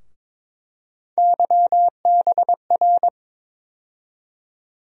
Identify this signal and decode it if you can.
Morse code, heard as a keyed tone, about 22 wpm, 700 Hz. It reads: YBR
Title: USA Call Sign Postfixes - 3 Letters 22wpm